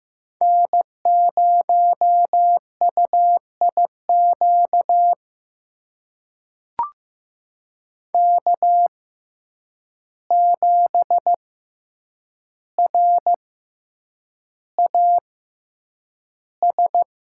Morse code, heard as a keyed tone, about 15 wpm, spaced as for 8 wpm, 700 Hz. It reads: N0UIQ K7RAS
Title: USA Call Signs - 5 Characters - 6x Character Spacing - 15wpm